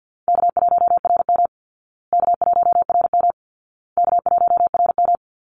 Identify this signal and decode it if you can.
Morse code, heard as a keyed tone, about 50 wpm, 700 Hz. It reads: K1RM K1RM K1RM
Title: Call Signs Repeated 2x - 50wpm